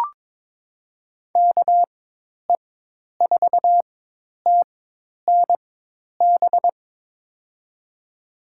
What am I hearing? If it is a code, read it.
Morse code, heard as a keyed tone, about 22 wpm, spaced as for 10 wpm, 700 Hz. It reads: KE4TNB
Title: USA Call Signs - 4x Character Spacing - 22wpm